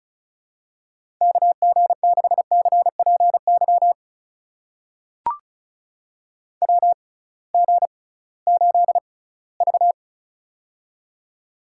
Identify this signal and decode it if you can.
Morse code, heard as a keyed tone, about 34 wpm, spaced as for 20 wpm, 700 Hz. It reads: KG6CPY WG8V
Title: USA Call Signs - 6x Character Spacing - 35wpm